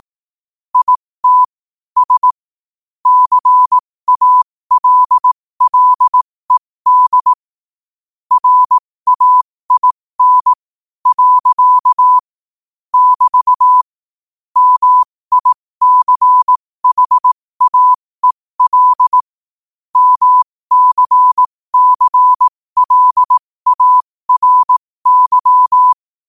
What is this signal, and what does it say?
Morse code, heard as a keyed tone, about 18 wpm, spaced as for 15 wpm, 1000 Hz. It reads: IT S CALLED RAIN . <BT> MICHAEL MCCLARY
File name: qotdfiles / 15wpm